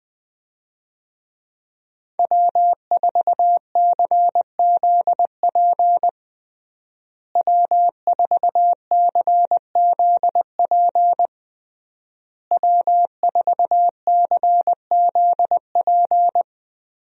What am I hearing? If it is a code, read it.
Morse code, heard as a keyed tone, about 20 wpm, 700 Hz. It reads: W4CZP W4CZP W4CZP
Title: USA Call Signs - 5 Characters - Repeated 3x - 20wpm